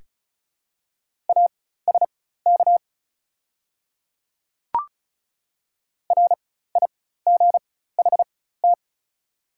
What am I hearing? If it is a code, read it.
Morse code, heard as a keyed tone, about 35 wpm, spaced as for 16 wpm, 700 Hz. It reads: ASK RIGHT
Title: 4x Character Spacing - Top 300 Words - 35wpm